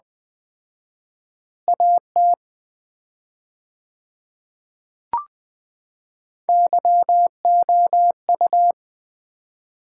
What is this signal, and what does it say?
Morse code, heard as a keyed tone, about 20 wpm, 700 Hz. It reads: AT YOU
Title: 4x Word-Spacing - Sets of 4 Words - 20wpm